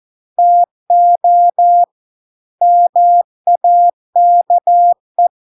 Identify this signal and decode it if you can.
Morse code, heard as a keyed tone, about 14 wpm, 700 Hz. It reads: TO MAKE